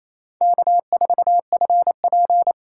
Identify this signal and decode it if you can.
Morse code, heard as a keyed tone, about 35 wpm, 700 Hz. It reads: K4FP